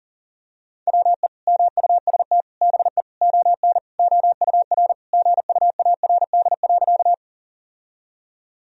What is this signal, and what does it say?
Morse code, heard as a keyed tone, about 37 wpm, 700 Hz. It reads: WE MUST BE ON OUR GUARD.
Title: Sentences from Top 1000 Words - Review Part 4 - 40wpm